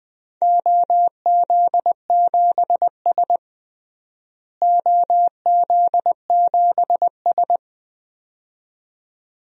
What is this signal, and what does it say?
Morse code, heard as a keyed tone, about 21 wpm, 700 Hz. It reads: OZ7S OZ7S
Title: DXCC-100 Call Signs - 4 Characters - Repeated 2x - 20wpm